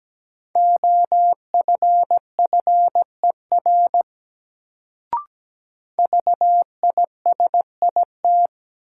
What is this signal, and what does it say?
Morse code, heard as a keyed tone, about 17 wpm, 700 Hz. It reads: OFFER VISIT
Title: Sentences from Top 900 Words - New Word Drill - 17wpm